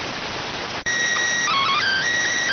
SSTV (slow-scan television), PD 290.